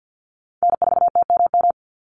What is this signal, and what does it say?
Morse code, heard as a keyed tone, about 50 wpm, 700 Hz. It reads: N4TMM
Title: Call Signs Repeated 3x - 50wpm